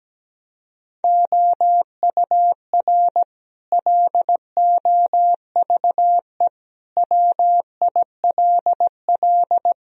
Morse code, heard as a keyed tone, about 17 wpm, 700 Hz. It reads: OUR LOVE WILL
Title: Sentences from Top 1000 Words - Review Part 3 - 17wpm